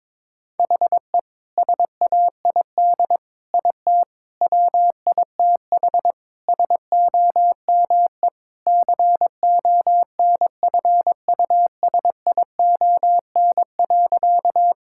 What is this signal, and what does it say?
Morse code, heard as a keyed tone, about 22 wpm, 700 Hz. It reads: HE SAID IT WITH SOME CONFUSION.
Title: Sentences from Top 1600 Words 22wpm